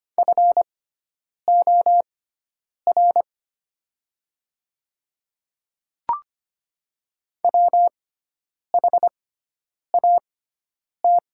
Morse code, heard as a keyed tone, about 25 wpm, spaced as for 9 wpm, 700 Hz. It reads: FOR WHAT
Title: 6x Character Spacing - Top 200 Words - 25wpm